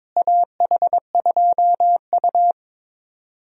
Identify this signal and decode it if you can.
Morse code, heard as a keyed tone, about 22 wpm, 700 Hz. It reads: AH2U
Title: USA Call Signs - 4 Characters - Repeated 3x - 22wpm